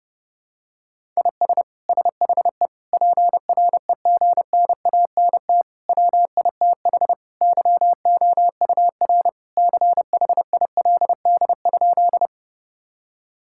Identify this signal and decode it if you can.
Morse code, heard as a keyed tone, about 32 wpm, 700 Hz. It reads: IS SHE PREGNANT WITH YOUR CHILD?
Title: Sentences from Top 1000 Words - Review Part 6 - 30wpm